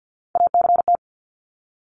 Morse code, heard as a keyed tone, about 50 wpm, 700 Hz. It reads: ACT
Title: Top 200 Words 50wpm